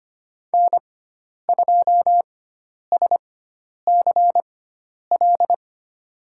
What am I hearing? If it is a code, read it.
Morse code, heard as a keyed tone, about 25 wpm, spaced as for 10 wpm, 700 Hz. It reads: N2SCL
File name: Lesson 16g - Letter c - Call Signs 25wpm